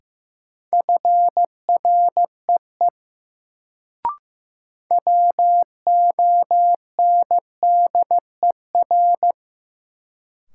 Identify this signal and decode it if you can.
Morse code, heard as a keyed tone, about 15 wpm, 700 Hz. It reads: FREE WONDER
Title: Sentences from Top 500 Words - New Word Drill - 15wpm